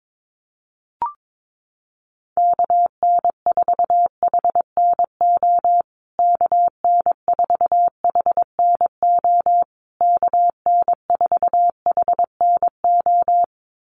Morse code, heard as a keyed tone, about 22 wpm, 700 Hz. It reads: KN4HNO KN4HNO KN4HNO
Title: USA Call Signs Repeated 3x 22wpm